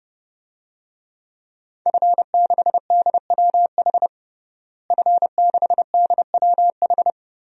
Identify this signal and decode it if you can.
Morse code, heard as a keyed tone, about 32 wpm, 700 Hz. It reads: F6DWH F6DWH
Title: DXCC-100 Call Signs - 5 Characters - Repeated 2x - 30wpm